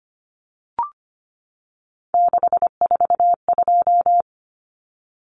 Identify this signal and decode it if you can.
Morse code, heard as a keyed tone, about 25 wpm, 700 Hz. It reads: 642
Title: Numbers - 3-Digit 25wpm